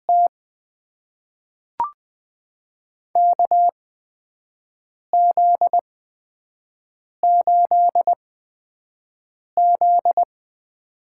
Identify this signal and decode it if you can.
Morse code, heard as a keyed tone, about 20 wpm, spaced as for 5 wpm, 700 Hz. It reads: T KZ8Z
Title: USA Call Signs - 4 Characters - 8x Character Spacing - 20wpm